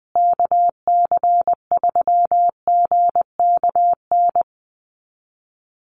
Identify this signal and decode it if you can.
Morse code, heard as a keyed tone, about 20 wpm, 700 Hz. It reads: KC3GKN
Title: USA Call Signs Repeated 2x 20wpm